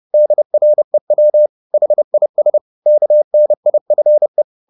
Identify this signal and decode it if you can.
Morse code, heard as a keyed tone, about 30 wpm, 600 Hz. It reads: DREW HIS KNIFE